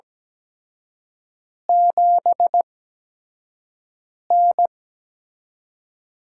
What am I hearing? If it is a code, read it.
Morse code, heard as a keyed tone, about 17 wpm, spaced as for 5 wpm, 700 Hz. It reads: 7N
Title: USA Call Signs - 8x Character Spacing - 17wpm